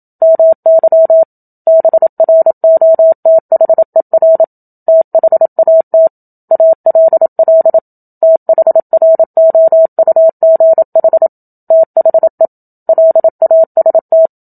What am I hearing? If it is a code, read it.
Morse code, heard as a keyed tone, about 27 wpm, 650 Hz. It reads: MY BROTHER THAT ALL THROUGH THE LAST
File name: Book Two / 01-Chapter 1 - 27 WPM